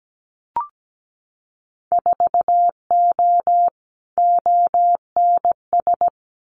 Morse code, heard as a keyed tone, about 17 wpm, 700 Hz. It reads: 4O ONS
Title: Winter Field Day Exchanges 17wpm